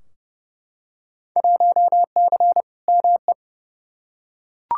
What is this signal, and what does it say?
Morse code, heard as a keyed tone, about 30 wpm, 700 Hz. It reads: 1C ME
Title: Field Day Exchanges - 30wpm